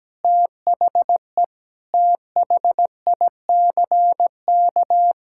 Morse code, heard as a keyed tone, about 17 wpm, 700 Hz. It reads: THE THICK
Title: Sentences from Top 1600 Words 17wpm